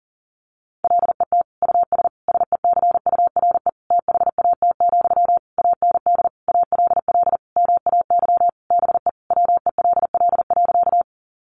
Morse code, heard as a keyed tone, about 40 wpm, 700 Hz. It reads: LET US SECURE THAT, AND ALL MAY BE WELL.
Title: Sentences from Top 1200 Words 40wpm